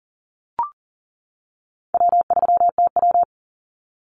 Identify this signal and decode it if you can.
Morse code, heard as a keyed tone, about 40 wpm, 700 Hz. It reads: W3TW
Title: Call Signs - 40wpm